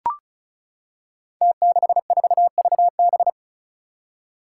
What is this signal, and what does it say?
Morse code, heard as a keyed tone, about 36 wpm, 700 Hz. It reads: T64VB
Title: Five Challenging Letter-Number Combinations 35wpm